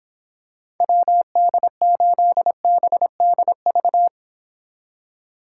Morse code, heard as a keyed tone, about 26 wpm, 700 Hz. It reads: WD8BDV